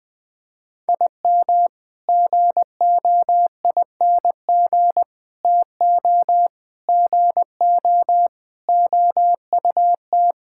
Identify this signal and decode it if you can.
Morse code, heard as a keyed tone, about 19 wpm, 700 Hz. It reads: IM GOING TO GO OUT
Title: Sentences with 8-Word Phrases - Spoken Beginning - 20wpm